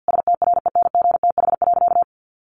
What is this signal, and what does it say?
Morse code, heard as a keyed tone, about 50 wpm, 700 Hz. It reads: STRENGTH.
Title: Sentences from Top 1300 Words 50wpm